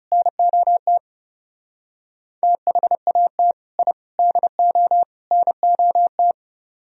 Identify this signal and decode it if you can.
Morse code, heard as a keyed tone, about 38 wpm, 700 Hz. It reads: NOT THAT I DO NOT